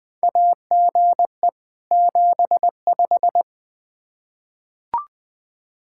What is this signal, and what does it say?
Morse code, heard as a keyed tone, about 20 wpm, 700 Hz. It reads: AGE 75
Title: Practice QSO - Exchange 3 - More Info - 20wpm